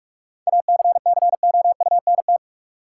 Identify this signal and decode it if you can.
Morse code, heard as a keyed tone, about 47 wpm, 700 Hz. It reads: ACCOUNT
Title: Sentences from Top 900 Words - New Word Drill - 45wpm